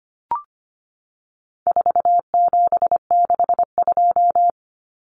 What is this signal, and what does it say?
Morse code, heard as a keyed tone, about 25 wpm, 700 Hz. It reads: 4762
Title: Numbers - 4-Digit 25wpm